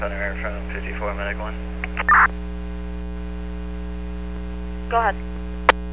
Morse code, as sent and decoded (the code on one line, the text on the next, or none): none